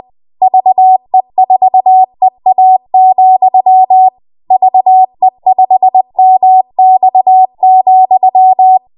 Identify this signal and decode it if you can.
Morse code, heard as a keyed tone, about 20 wpm, 750 Hz. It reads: VE4EA, VE5MX,
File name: Morse / 20WPM